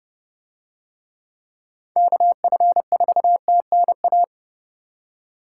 Morse code, heard as a keyed tone, about 26 wpm, 700 Hz. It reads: KF4TNA